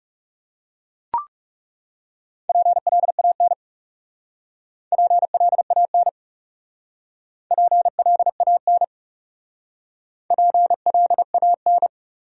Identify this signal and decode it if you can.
Morse code, heard as a keyed tone, about 36 wpm, 700 Hz. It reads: PLAN PLAN PLAN PLAN
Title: Speed-Racing Top 500 Words 30wpm